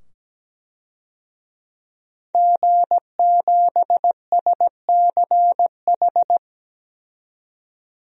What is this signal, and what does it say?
Morse code, heard as a keyed tone, about 17 wpm, 700 Hz. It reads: G7SCH
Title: DXCC-100 Call Signs - 5 Characters - 17wpm